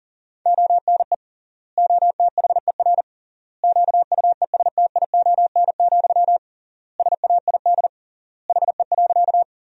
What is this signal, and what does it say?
Morse code, heard as a keyed tone, about 40 wpm, 700 Hz. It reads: ONE OTHER QUESTION, SAID HE.
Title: 3x Word-Spacing - Sentences from Top 400 Words - 40wpm